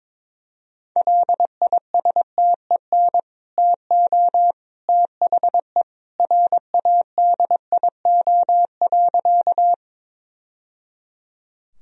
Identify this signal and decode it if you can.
Morse code, heard as a keyed tone, about 22 wpm, 700 Hz. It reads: LISTEN TO THE RADIO.
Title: Sentences from Top 1000 Words - Review Part 3 - 22wpm